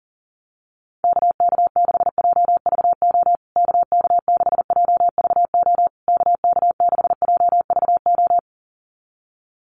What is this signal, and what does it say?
Morse code, heard as a keyed tone, about 40 wpm, 700 Hz. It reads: KK6JVO KK6JVO KK6JVO
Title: USA Call Signs Repeated 3x 40wpm